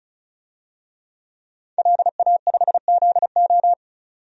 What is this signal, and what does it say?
Morse code, heard as a keyed tone, about 36 wpm, 700 Hz. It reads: LA5ZO
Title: DXCC-100 Call Signs - 5 Characters - 35wpm